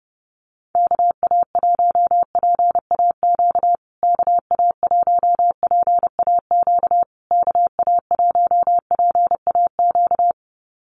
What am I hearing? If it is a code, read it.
Morse code, heard as a keyed tone, about 30 wpm, 700 Hz. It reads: KA1PAQ KA1PAQ KA1PAQ
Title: USA Call Signs Repeated 3x 30wpm